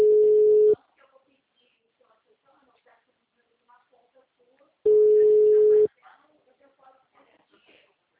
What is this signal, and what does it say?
Ringback tone (Europe, 425 Hz), with speech.
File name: out- -2033- - - .30009